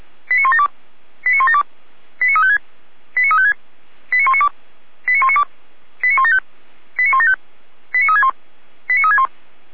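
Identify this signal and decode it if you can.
Five-tone selective call (ZVEI), heard as quick runs of five tones, tones of about 80 ms. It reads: 89172 89172 89356 89356 89192 89192 89175 89175 89361 89361
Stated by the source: ZVEI five-tone selective call 89018 received by SDR.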